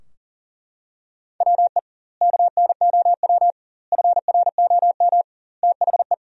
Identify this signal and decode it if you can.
Morse code, heard as a keyed tone, about 40 wpm, 700 Hz. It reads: WE KNOW FROM THE
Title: 2x Word-Spacing - Sentences from Top 500 Words - 40wpm